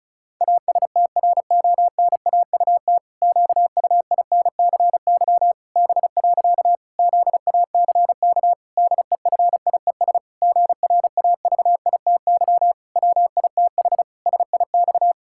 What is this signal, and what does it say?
Morse code, heard as a keyed tone, about 36 wpm, 700 Hz. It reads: ASTRONAUT QUINCY B. ZACK DEFIES GRAVITY WITH SIX
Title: Pangram Challenge v2 - 35wpm